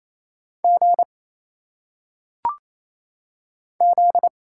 Morse code, heard as a keyed tone, about 27 wpm, 700 Hz. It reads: G Z
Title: Single Letters - Phonetic Pronunciation 28wpm